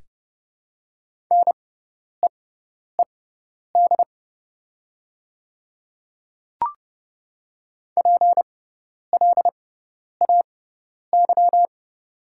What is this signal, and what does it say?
Morse code, heard as a keyed tone, about 30 wpm, spaced as for 10 wpm, 700 Hz. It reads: NEED PLAY
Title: 6x Character Spacing - Top 200 Words - 30wpm